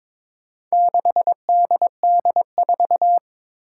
Morse code, heard as a keyed tone, about 23 wpm, 700 Hz. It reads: 6DD4
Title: Four Challenging Letter-Number Combinations 22wpm